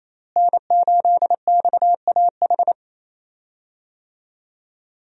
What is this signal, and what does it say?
Morse code, heard as a keyed tone, about 27 wpm, 700 Hz. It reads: N8XAH
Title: USA Call Signs - 5 Characters - Repeated 3x - 28wpm